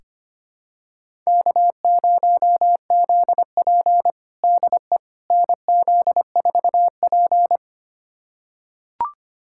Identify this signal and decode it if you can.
Morse code, heard as a keyed tone, about 25 wpm, 700 Hz. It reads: K0ZP DE NZ4P
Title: USA Callsign DE Callsign - 4 Characters - 25wpm